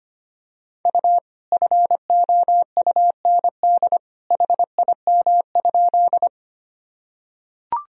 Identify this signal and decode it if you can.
Morse code, heard as a keyed tone, about 25 wpm, 700 Hz. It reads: U FOUND HIM?
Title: Sentences from Top 300 Words and Abbreviations 25wpm